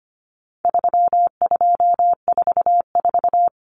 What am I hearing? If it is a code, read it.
Morse code, heard as a keyed tone, about 25 wpm, 700 Hz. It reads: 3244
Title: Common Cut Numbers - 4-Digit 25wpm